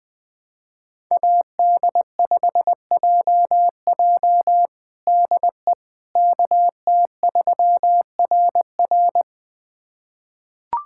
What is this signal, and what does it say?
Morse code, heard as a keyed tone, about 21 wpm, 700 Hz. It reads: AD5JJ DE KT3RR
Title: USA Callsign DE Callsign - 5 Characters - 20wpm